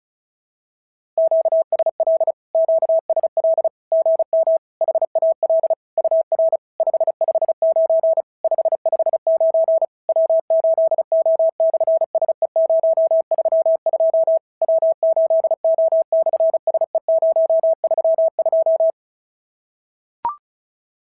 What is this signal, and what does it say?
Morse code, heard as a keyed tone, about 36 wpm, 650 Hz. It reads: QSL QSL GM HAL UR 559 559 W8O/SE032 W8O/SE032
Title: SOTA QSO Practice 35wpm